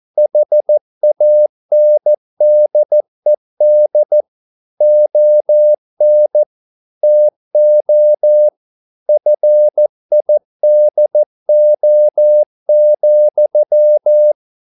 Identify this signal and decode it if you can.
Morse code, heard as a keyed tone, about 14 wpm, 600 Hz. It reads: HANDED ON TO FIDO,
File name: mp3 files / a dogs mistake - 14wpm